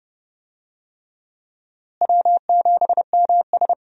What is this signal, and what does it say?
Morse code, heard as a keyed tone, about 30 wpm, 700 Hz. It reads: W7MS
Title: USA Call Signs - 4 Characters - Repeated 3x - 30wpm